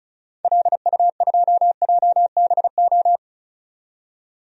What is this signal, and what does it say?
Morse code, heard as a keyed tone, about 36 wpm, 690 Hz. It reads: LU2JBO
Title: DXCC-100 Call Signs - 6 Characters - 35wpm